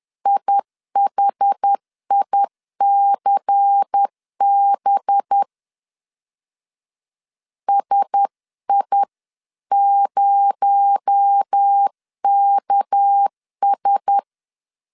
Morse code, heard as a keyed tone, about 10 wpm, 800 Hz. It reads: IHICB SI0KS